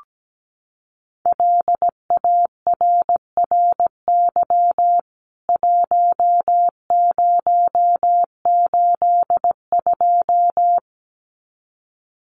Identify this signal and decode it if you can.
Morse code, heard as a keyed tone, about 17 wpm, 700 Hz. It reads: LARRY 1082
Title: CWT Exchange 17wpm